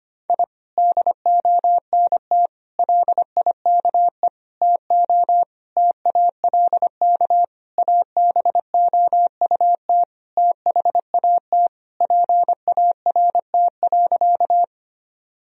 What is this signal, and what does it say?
Morse code, heard as a keyed tone, about 25 wpm, 700 Hz. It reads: I DONT LIKE TO TALK ABOUT THAT PART.
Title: Sentences from Top 1000 Words - Review Part 1 - 25wpm